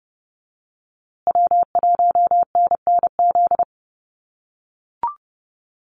Morse code, heard as a keyed tone, about 30 wpm, 700 Hz. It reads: W1NNZ
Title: USA Call Signs Repeated 3x 30wpm